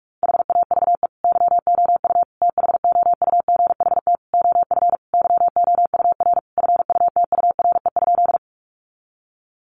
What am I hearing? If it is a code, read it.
Morse code, heard as a keyed tone, about 44 wpm, 700 Hz. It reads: HAVE YOU THOUGHT OF YOUR FUTURE?
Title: Sentences from Top 700 Words 45wpm